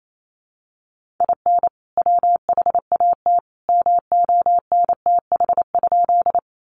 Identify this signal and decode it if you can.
Morse code, heard as a keyed tone, about 28 wpm, 700 Hz. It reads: IN WHAT MONTH?
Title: Sentences from Top 800 Words 28wpm